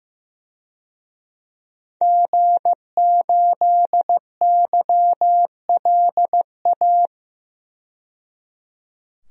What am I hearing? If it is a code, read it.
Morse code, heard as a keyed tone, about 15 wpm, 700 Hz. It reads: G8YLA
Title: DXCC-100 Call Signs - 5 Characters - 15wpm